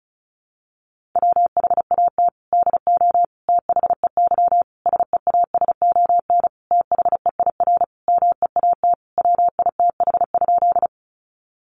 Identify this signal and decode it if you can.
Morse code, heard as a keyed tone, about 34 wpm, 700 Hz. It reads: WHAT DO THEY SEASON THEIR MEAT WITH?
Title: Sentences from Top 1400 Words 35wpm